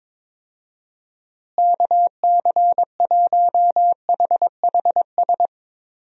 Morse code, heard as a keyed tone, about 21 wpm, 700 Hz. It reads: KC1HHS
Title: USA Call Signs - 6 Characters - Repeated 3x - 22wpm